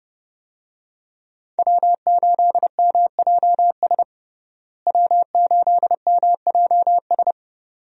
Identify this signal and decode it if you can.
Morse code, heard as a keyed tone, about 30 wpm, 700 Hz. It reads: W8MJS W8MJS
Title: USA Call Signs - 5 Characters - Repeated 2x - 30wpm